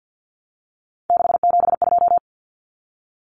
Morse code, heard as a keyed tone, about 50 wpm, 700 Hz. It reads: 672
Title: Numbers - 3-Digit 50wpm